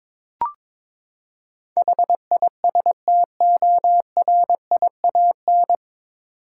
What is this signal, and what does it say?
Morse code, heard as a keyed tone, about 22 wpm, 700 Hz. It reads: HISTORIAN
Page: Morse Code Ninja practice 22wpm